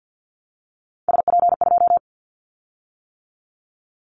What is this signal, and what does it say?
Morse code, heard as a keyed tone, about 50 wpm, 700 Hz. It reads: SP2
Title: Callsign Prefixes for Top 25 DXCC - 50wpm